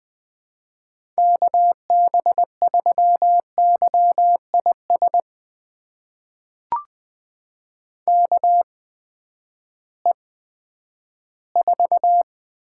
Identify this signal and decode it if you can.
Morse code, heard as a keyed tone, about 20 wpm, spaced as for 11 wpm, 700 Hz. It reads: KB3YIS KE4
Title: USA Call Signs - 6 Characters - 8x Character Spacing - 20wpm